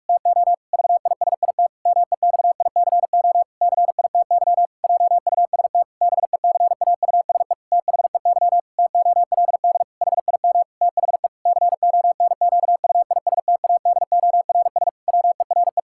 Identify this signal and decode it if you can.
Morse code, heard as a keyed tone, about 46 wpm, 700 Hz. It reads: TO VISIT MEXICO CITY JUST BECAUSE THEY TOLD HIM THE CONQUISTADORS WERE
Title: Pangram Challenge v2 - 45wpm